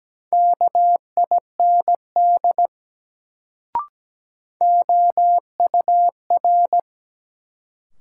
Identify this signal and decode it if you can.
Morse code, heard as a keyed tone, about 17 wpm, 700 Hz. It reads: KIND OUR
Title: Sets of 2 Words - New Word Drill - 17wpm